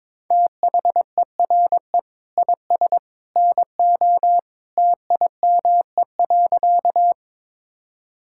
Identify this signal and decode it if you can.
Morse code, heard as a keyed tone, about 22 wpm, 700 Hz. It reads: THERE IS NO TIME.
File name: practice / 22wpm / Sentences with 2-Word Phrases - 22wpm